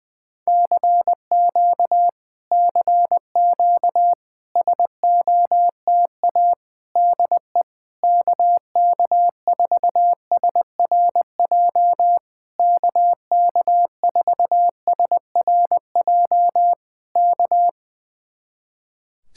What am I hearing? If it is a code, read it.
Morse code, heard as a keyed tone, about 20 wpm, 700 Hz. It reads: CQ CQ SOTA DE KK4SRJ KK4SRJ K
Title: Speed-Racing Common QSO Phrases - 20wpm